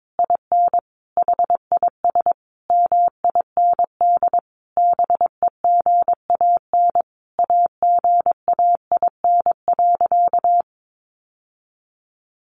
Morse code, heard as a keyed tone, about 22 wpm, 700 Hz. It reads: IN HIS MIND BEGAN AGAIN.
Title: Sentences from Top 1300 Words 22wpm